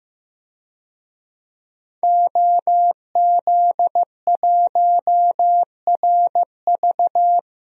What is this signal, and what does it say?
Morse code, heard as a keyed tone, about 15 wpm, 700 Hz. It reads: OZ1RV
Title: DXCC-100 Call Signs - 5 Characters - Repeated 2x - 15wpm